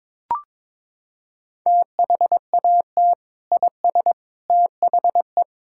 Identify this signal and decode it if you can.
Morse code, heard as a keyed tone, about 22 wpm, 700 Hz. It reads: THAT IS THE
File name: practice / 22wpm / Sentences with 3-Word Phrases - 22wpm